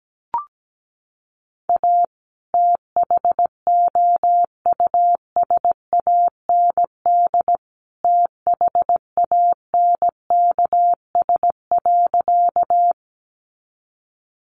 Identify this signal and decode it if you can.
Morse code, heard as a keyed tone, about 17 wpm, 700 Hz. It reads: A THOUSAND THANKS.
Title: Sentences from Top 600 Words 17wpm